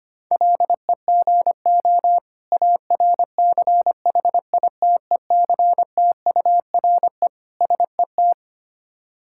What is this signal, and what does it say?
Morse code, heard as a keyed tone, about 25 wpm, 690 Hz. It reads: LEGO ARCHITECTURE SET
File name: practice / 25wpm / Difficult Words with Hints - 25wpm